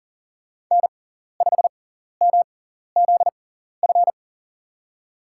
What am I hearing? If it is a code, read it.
Morse code, heard as a keyed tone, about 39 wpm, spaced as for 14 wpm, 700 Hz. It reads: N5MZF
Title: USA Call Signs - 5 Characters - 6x Character Spacing - 40wpm